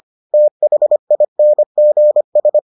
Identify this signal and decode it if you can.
Morse code, heard as a keyed tone, about 25 wpm, 600 Hz. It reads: THINGS